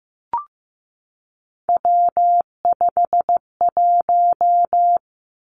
Morse code, heard as a keyed tone, about 15 wpm, 700 Hz. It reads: W51
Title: Three Challenging Letter-Number Combinations 15wpm